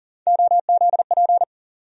Morse code, heard as a keyed tone, about 40 wpm, 700 Hz. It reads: OZP
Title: USA Call Sign Postfixes - 3 Letters 40wpm